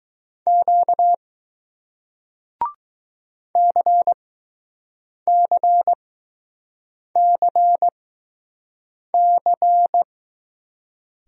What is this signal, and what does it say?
Morse code, heard as a keyed tone, about 20 wpm, spaced as for 6 wpm, 700 Hz. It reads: Q CCCC